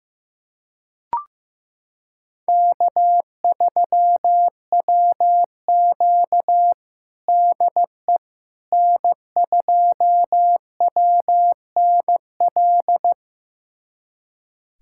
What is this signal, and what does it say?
Morse code, heard as a keyed tone, about 15 wpm, 700 Hz. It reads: K3WQ DE N2WNL
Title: USA Callsign DE Callsign 15wpm